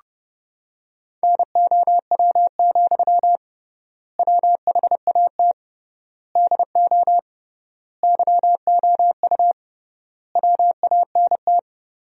Morse code, heard as a keyed tone, about 30 wpm, 700 Hz. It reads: NOW, WHAT DO YOU WANT
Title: 3x Word-Spacing - Sentences from Top 200 Words - 30wpm